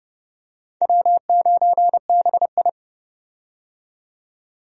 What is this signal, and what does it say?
Morse code, heard as a keyed tone, about 30 wpm, 700 Hz. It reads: W9BI
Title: USA Call Signs - 4 Characters - 30wpm